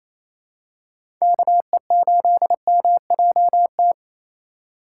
Morse code, heard as a keyed tone, about 28 wpm, 700 Hz. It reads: KE8MJT
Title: USA Call Signs - 2x Character Spacing - 28wpm